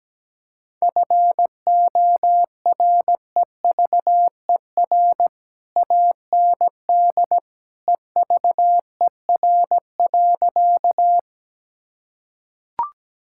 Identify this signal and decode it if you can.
Morse code, heard as a keyed tone, about 17 wpm, 700 Hz. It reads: FOREVER AND EVER.
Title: Sentences from Top 1600 Words 17wpm